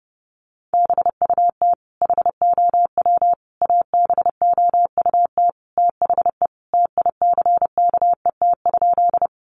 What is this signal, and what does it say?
Morse code, heard as a keyed tone, about 30 wpm, 700 Hz. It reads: BUT HOW ABOUT THE TICKET?
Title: Sentences from Top 1400 Words 30wpm